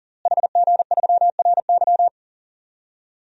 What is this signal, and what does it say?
Morse code, heard as a keyed tone, about 40 wpm, 700 Hz. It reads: HG3RY
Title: DXCC-100 Call Signs - 5 Characters - Repeated 3x - 40wpm